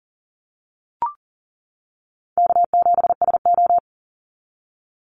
Morse code, heard as a keyed tone, about 40 wpm, 700 Hz. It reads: K7SO
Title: NA SOTA Activator Callsigns 40wpm